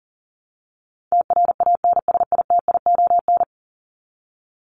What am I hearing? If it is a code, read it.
Morse code, heard as a keyed tone, about 39 wpm, 700 Hz. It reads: TRANSITION
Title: Top 1751-2000 Words 40wpm